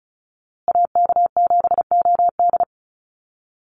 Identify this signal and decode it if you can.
Morse code, heard as a keyed tone, about 34 wpm, 700 Hz. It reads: AK7OD
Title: USA Call Signs Repeated 3x 35wpm